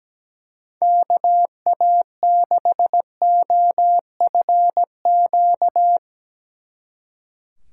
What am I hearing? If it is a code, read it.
Morse code, heard as a keyed tone, about 19 wpm, 700 Hz. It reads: KA6OFQ
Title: Speed Racing - USA 2x3 Call Signs 17wpm